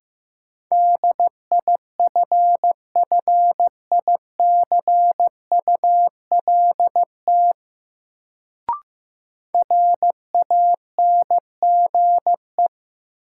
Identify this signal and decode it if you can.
Morse code, heard as a keyed tone, about 15 wpm, 700 Hz. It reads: DIFFICULT RANGE
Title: Sets of 4 Words - New Word Drill - 15wpm